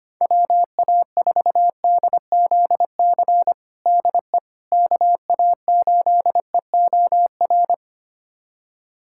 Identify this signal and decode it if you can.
Morse code, heard as a keyed tone, about 25 wpm, 700 Hz. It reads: WA4DZC DE KA8EOR
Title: USA Callsign DE Callsign - 6 Characters - 25wpm